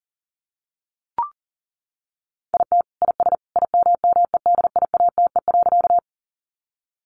Morse code, heard as a keyed tone, about 40 wpm, 700 Hz. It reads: IT IS IMMEDIATE.
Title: Sentences from Top 1200 Words 40wpm